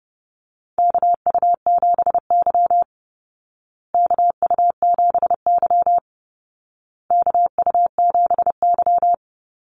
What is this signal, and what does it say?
Morse code, heard as a keyed tone, about 30 wpm, 700 Hz. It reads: KU7Y KU7Y KU7Y
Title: Call Signs Repeated 2x - 30wpm